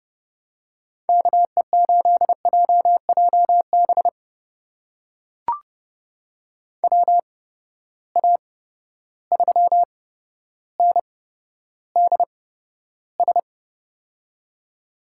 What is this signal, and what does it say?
Morse code, heard as a keyed tone, about 30 wpm, spaced as for 13 wpm, 700 Hz. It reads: KE8JJB WA3NDS
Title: USA Call Signs - 6 Characters - 8x Character Spacing - 30wpm